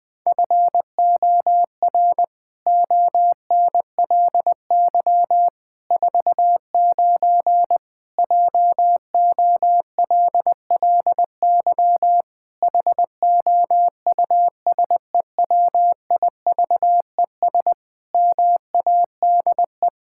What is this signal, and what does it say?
Morse code, heard as a keyed tone, about 19 wpm, 700 Hz. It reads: FOR ONLY 49 JOLLY HOUSEWIVES MADE
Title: Pangram Challenge v2 - 20wpm